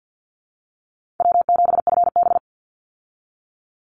Morse code, heard as a keyed tone, about 50 wpm, 700 Hz. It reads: W7FB
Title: Call Signs - 50wpm